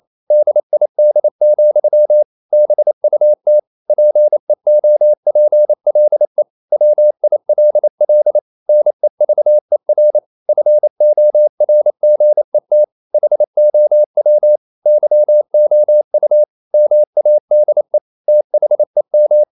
Morse code, heard as a keyed tone, about 28 wpm, 600 Hz. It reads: DID, BUT PEOPLE WILL NEVER FORGET HOW YOU MADE THEM
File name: CW / Practice / EnglishQuotations0228